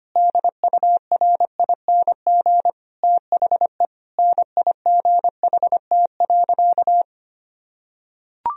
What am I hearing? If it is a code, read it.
Morse code, heard as a keyed tone, about 25 wpm, 700 Hz. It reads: DURING THE NIGHT.
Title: Sentences from Top 1000 Words - Review Part 4 - 25wpm